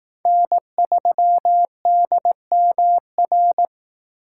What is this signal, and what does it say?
Morse code, heard as a keyed tone, about 19 wpm, 700 Hz. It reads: N3DMR